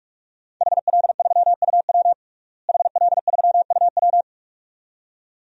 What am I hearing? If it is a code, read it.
Morse code, heard as a keyed tone, about 46 wpm, 700 Hz. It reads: HL3UW HL3UW
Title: DXCC-100 Call Signs - 5 Characters - Repeated 2x - 45wpm